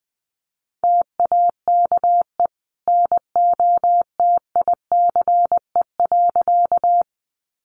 Morse code, heard as a keyed tone, about 20 wpm, 700 Hz. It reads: TAKE NOTICE.
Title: Sentences from Top 400 Words 20wpm